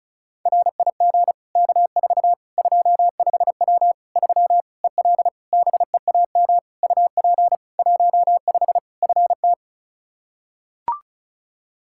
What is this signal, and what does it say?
Morse code, heard as a keyed tone, about 36 wpm, 700 Hz. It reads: RIG K4 25W 3 EL BEAM UP 15 FT
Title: Practice QSO - Exchange 2 - Info - 35wpm